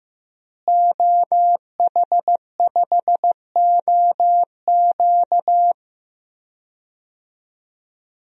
Morse code, heard as a keyed tone, about 16 wpm, 700 Hz. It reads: OH5OQ
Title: DXCC-100 Call Signs - 5 Characters - Repeated 3x - 15wpm